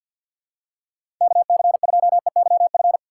Morse code, heard as a keyed tone, about 54 wpm, 700 Hz. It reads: K C 2 E Y F
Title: USA Call Signs - 6 Characters - 8x Character Spacing - 50wpm